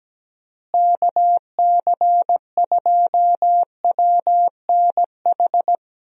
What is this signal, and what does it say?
Morse code, heard as a keyed tone, about 17 wpm, 700 Hz. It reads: KC2WNH
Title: USA Call Signs - 6 Characters - Repeated 3x - 17wpm